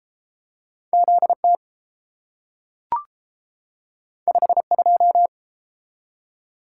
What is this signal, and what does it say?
Morse code, heard as a keyed tone, about 30 wpm, 700 Hz. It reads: Z T 5 2